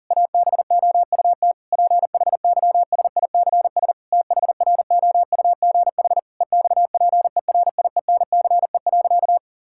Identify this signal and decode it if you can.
Morse code, heard as a keyed tone, about 40 wpm, 700 Hz. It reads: ABOUT PHYSICS THROUGH EXPERIENCE.
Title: Sentences from Top 1000 Words - Review Part 5 - 40wpm